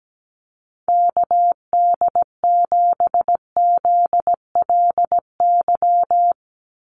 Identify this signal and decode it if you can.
Morse code, heard as a keyed tone, about 17 wpm, 700 Hz. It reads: KD7ZLY
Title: USA Call Signs 17wpm